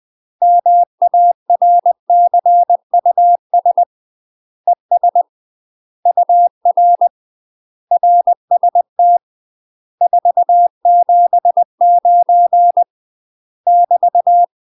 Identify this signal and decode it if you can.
Morse code, heard as a keyed tone, about 20 wpm, 700 Hz. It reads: MARCUS ES UR RST 479 <BT>